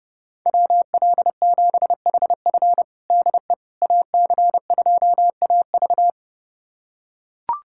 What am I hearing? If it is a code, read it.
Morse code, heard as a keyed tone, about 32 wpm, 700 Hz. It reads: WL7HF DE AC2AV
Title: USA Callsign DE Callsign - 5 Characters - 30wpm